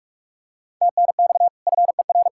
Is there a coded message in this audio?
Yes: Morse code from 0.8 to 2.3 s, heard as a keyed tone, about 44 wpm, 700 Hz. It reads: TNX FER